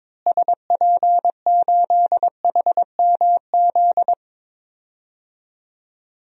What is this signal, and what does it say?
Morse code, heard as a keyed tone, about 22 wpm, 700 Hz. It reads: SP8HMZ
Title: DXCC-100 Call Signs - 6 Characters - Repeated 3x - 22wpm